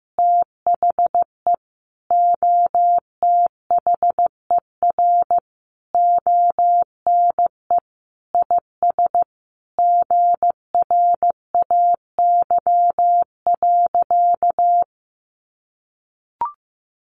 Morse code, heard as a keyed tone, about 15 wpm, 700 Hz. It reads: THE OTHER ONE IS GRAY.
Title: Additional Easy Phrases 15wpm